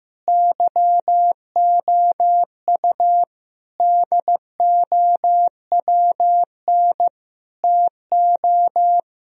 Morse code, heard as a keyed tone, about 15 wpm, 700 Hz. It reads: YOU DOWN TO
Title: Sentences from Top 1000 Words - Review Part 2 - 15wpm